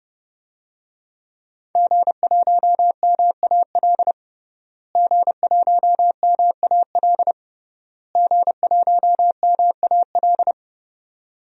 Morse code, heard as a keyed tone, about 29 wpm, 700 Hz. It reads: G1MAL G1MAL G1MAL
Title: DXCC-100 Call Signs - 5 Characters - Repeated 3x - 30wpm